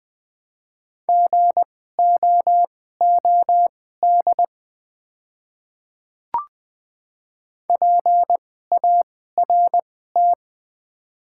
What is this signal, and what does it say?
Morse code, heard as a keyed tone, about 20 wpm, spaced as for 14 wpm, 700 Hz. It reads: GOOD PART
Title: 2x Character Spacing - Top 300 Words - 20wpm